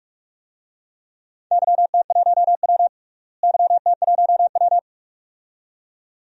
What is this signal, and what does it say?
Morse code, heard as a keyed tone, about 46 wpm, 700 Hz. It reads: YT1W YT1W
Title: DXCC-100 Call Signs - 4 Characters - Repeated 2x - 45wpm